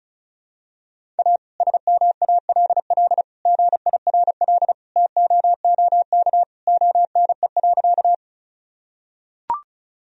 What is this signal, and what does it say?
Morse code, heard as a keyed tone, about 34 wpm, 700 Hz. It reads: A SMALL GIRL TOOK ONE.
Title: Sentences from Top 1000 Words - Review Part 1 - 35wpm